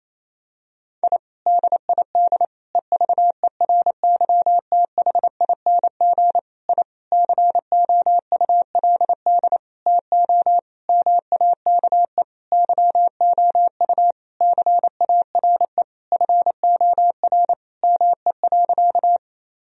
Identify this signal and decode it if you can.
Morse code, heard as a keyed tone, about 28 wpm, 700 Hz. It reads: I DID EVERYTHING I COULD TO MAKE YOU CARE FOR ME.
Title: Sentences from Top 1000 Words - Review Part 1 - 28wpm